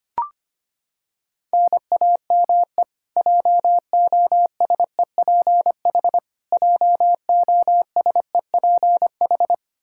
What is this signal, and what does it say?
Morse code, heard as a keyed tone, about 25 wpm, 700 Hz. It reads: NAME JOSEPH JOSEPH
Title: Practice QSO - Exchange 1 - Beginning - 25wpm